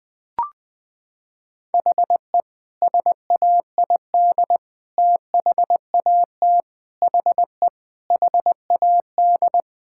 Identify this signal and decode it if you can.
Morse code, heard as a keyed tone, about 20 wpm, 700 Hz. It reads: HE SAID THAT HE HAD
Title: Sentences with 3-Word Phrases - 20wpm